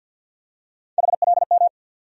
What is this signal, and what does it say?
Morse code, heard as a keyed tone, about 50 wpm, 690 Hz. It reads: HLM
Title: USA Call Sign Postfixes - 3 Letters 50wpm